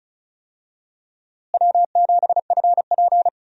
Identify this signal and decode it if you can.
Morse code, heard as a keyed tone, about 36 wpm, 700 Hz. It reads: W7FP
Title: USA Call Signs - 4 Characters - Repeated 3x - 35wpm